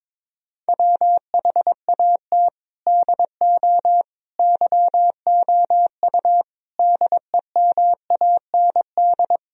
Morse code, heard as a keyed tone, about 22 wpm, 700 Hz. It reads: WHAT DO YOU DEMAND
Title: Sentences from Top 1000 Words - Review Part 4 - 22wpm